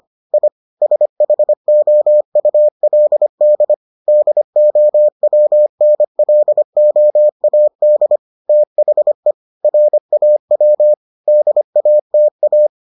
Morse code, heard as a keyed tone, about 25 wpm, 600 Hz. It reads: I SHOULD DOWNLOAD THE RAW DATA